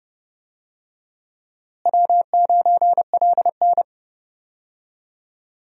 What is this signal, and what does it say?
Morse code, heard as a keyed tone, about 32 wpm, 700 Hz. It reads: W9LN
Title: USA Call Signs - 4 Characters - Repeated 3x - 30wpm